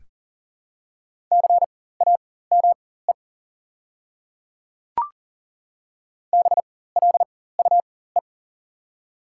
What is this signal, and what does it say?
Morse code, heard as a keyed tone, about 40 wpm, spaced as for 19 wpm, 700 Hz. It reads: CAME BLUE
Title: 4x Character Spacing - Top 500 Words - 40wpm